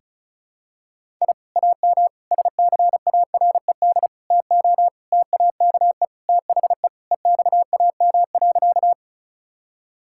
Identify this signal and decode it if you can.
Morse code, heard as a keyed tone, about 34 wpm, 700 Hz. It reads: I AM SCARED TO TAKE THE EXAM.
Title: Sentences from Top 1000 Words - Review Part 5 - 35wpm